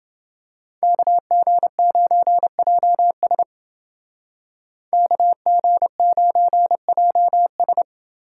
Morse code, heard as a keyed tone, about 28 wpm, 700 Hz. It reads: KG9JS KG9JS